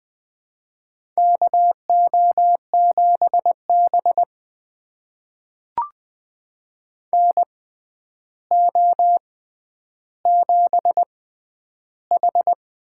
Morse code, heard as a keyed tone, about 20 wpm, spaced as for 10 wpm, 700 Hz. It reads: KO7B NO7H
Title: USA Call Signs - 6x Character Spacing - 20wpm